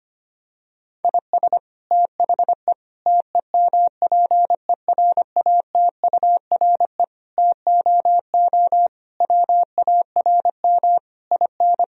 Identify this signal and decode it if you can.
Morse code, heard as a keyed tone, about 25 wpm, 700 Hz. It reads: IS THE TEMPERATURE TOO WARM IN
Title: Sentences from Top 1000 Words - Review Part 4 - 25wpm